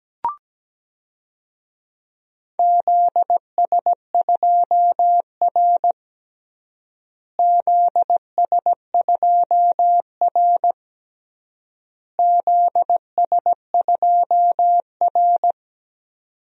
Morse code, heard as a keyed tone, about 17 wpm, 700 Hz. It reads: ZS2R ZS2R ZS2R
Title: DXCC-100 Call Signs - 4 Characters - Repeated 3x - 17wpm